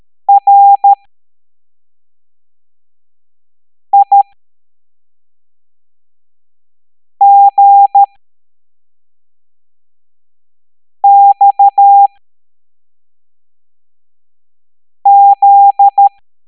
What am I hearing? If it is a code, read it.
Morse code, heard as a keyed tone, about 13 wpm, spaced as for 3 wpm, 800 Hz. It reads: RIGXZ